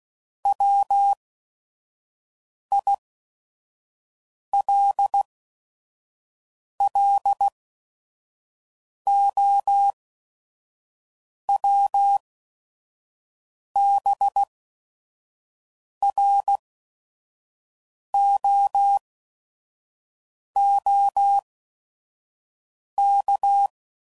Morse code, heard as a keyed tone, about 16 wpm, spaced as for 5 wpm, 800 Hz. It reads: WILLOWBROOK
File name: CW Code Groups @5WPM / fromSM's / Cities